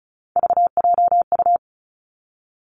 Morse code, heard as a keyed tone, about 36 wpm, 700 Hz. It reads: VJU
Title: Three Challenging Letter-Number Combinations 35wpm